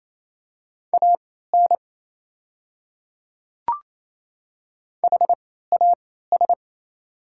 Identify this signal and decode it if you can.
Morse code, heard as a keyed tone, about 28 wpm, spaced as for 16 wpm, 690 Hz. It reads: AN HAS
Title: 3x Character Spacing - Top 100 Words - 28wpm